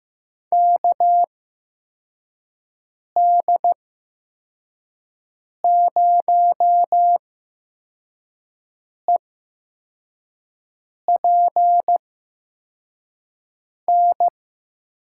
Morse code, heard as a keyed tone, about 15 wpm, spaced as for 4 wpm, 700 Hz. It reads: KD0EPN
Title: USA Call Signs - 8x Character Spacing - 15wpm